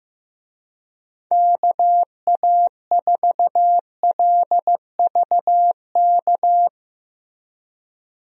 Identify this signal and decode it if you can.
Morse code, heard as a keyed tone, about 16 wpm, 700 Hz. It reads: KA4LVK
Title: USA Call Signs - 6 Characters - Repeated 3x - 15wpm